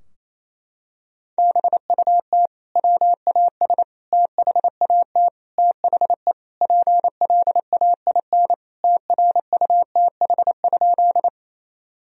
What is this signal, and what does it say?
Morse code, heard as a keyed tone, about 28 wpm, 700 Hz. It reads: BUT WAS THAT THE PLAIN TRUTH?
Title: Sentences from Top 1000 Words - Review Part 2 - 28wpm